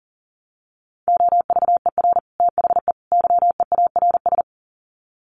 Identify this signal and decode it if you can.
Morse code, heard as a keyed tone, about 40 wpm, 700 Hz. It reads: OVER THE YEARS
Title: Sets of 3 Words 40wpm